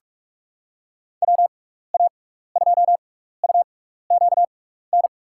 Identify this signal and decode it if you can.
Morse code, heard as a keyed tone, about 44 wpm, spaced as for 15 wpm, 700 Hz. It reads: WA2UQN
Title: USA Call Signs - 6 Characters - 6x Character Spacing - 45wpm